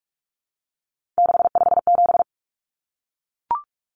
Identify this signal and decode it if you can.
Morse code, heard as a keyed tone, about 44 wpm, 700 Hz. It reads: 657
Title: Common Cut Numbers - 3-Digit 45wpm